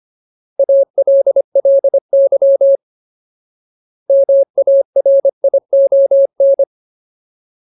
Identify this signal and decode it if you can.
Morse code, heard as a keyed tone, about 25 wpm, 550 Hz. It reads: ALLY MARION
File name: Ultimate Word List / 25WPM Ultimate